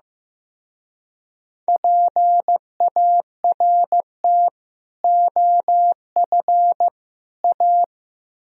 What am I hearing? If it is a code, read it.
Morse code, heard as a keyed tone, about 18 wpm, 700 Hz. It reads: PART OF A
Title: Speed-Racing - Sets of 3 Words 15wpm